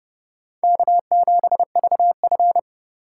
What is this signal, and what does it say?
Morse code, heard as a keyed tone, about 38 wpm, 700 Hz. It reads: K7VF